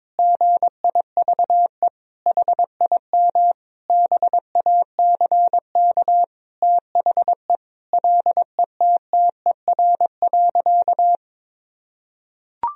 Morse code, heard as a keyed tone, about 22 wpm, 700 Hz. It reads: GIVE HIM BACK THE LETTER.
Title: Sentences from Top 1000 Words - Review Part 4 - 22wpm